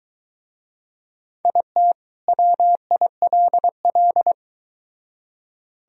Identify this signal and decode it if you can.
Morse code, heard as a keyed tone, about 24 wpm, 700 Hz. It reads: IT WILL